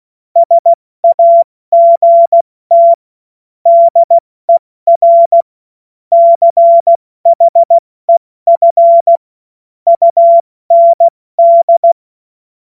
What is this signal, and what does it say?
Morse code, heard as a keyed tone, about 16 wpm, spaced as for 14 wpm, 680 Hz. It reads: SAGT DER CHEF UND
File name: Morsekurs / Witze / Witz 20. Januar 2026 70 F80